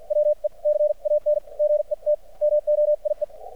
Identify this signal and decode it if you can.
Morse code, heard as a keyed tone, about 53 wpm, spaced as for 44 wpm, 600 Hz, with radio noise. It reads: WE CAN GET MOIE
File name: archived / CRSnet